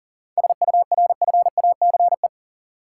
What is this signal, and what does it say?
Morse code, heard as a keyed tone, about 40 wpm, 700 Hz. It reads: SURFACE
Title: Sets of 4 Words - New Word Drill - 40wpm